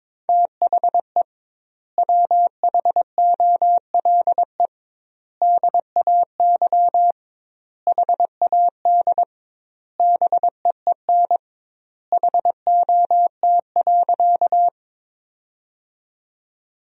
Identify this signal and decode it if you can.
Morse code, heard as a keyed tone, about 23 wpm, 700 Hz. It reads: THE WHOLE DAY HAD BEEN HOT.
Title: 2x Word-Spacing - Sentences from Top 500 Words - 22wpm